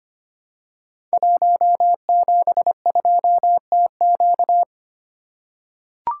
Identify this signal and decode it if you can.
Morse code, heard as a keyed tone, about 25 wpm, 700 Hz. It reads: 172TQ
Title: Code Groups - 5 Letter:Numbers - 25wpm